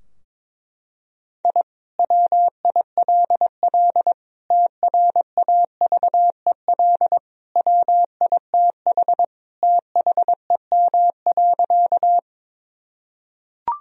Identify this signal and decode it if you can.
Morse code, heard as a keyed tone, about 22 wpm, 700 Hz. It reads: I WILL TRAVEL WITH THEM.
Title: Sentences from Top 1000 Words - Review Part 2 - 22wpm